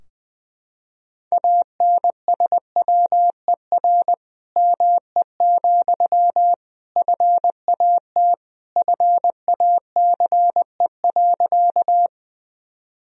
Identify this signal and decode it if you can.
Morse code, heard as a keyed tone, about 20 wpm, 700 Hz. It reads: ANSWER ME, FAT FACE.
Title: Sentences from Top 1000 Words - Review Part 4 - 20wpm